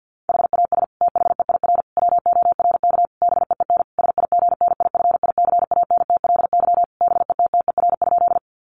Morse code, heard as a keyed tone, about 50 wpm, 700 Hz. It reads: HAS THEIR WORK BEEN SIGNIFICANTLY BETTER?
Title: Sentences from Top 1500 Words 50wpm